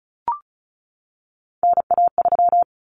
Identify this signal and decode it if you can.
Morse code, heard as a keyed tone, about 34 wpm, 700 Hz. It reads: NA3
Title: Common Cut Numbers - 3-Digit 35wpm